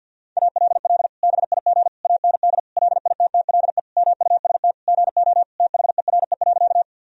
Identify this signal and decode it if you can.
Morse code, heard as a keyed tone, about 51 wpm, 700 Hz. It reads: ALL BIG AND LITTLE MUST GO THERE.
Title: Sentences from Top 1000 Words - Review Part 1 - 50wpm